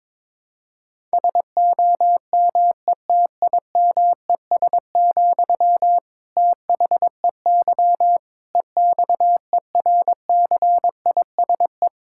Morse code, heard as a keyed tone, about 22 wpm, 700 Hz. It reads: SOMETIMES, THEY EXERCISE
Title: Sentences from Top 1000 Words - Review Part 6 - 22wpm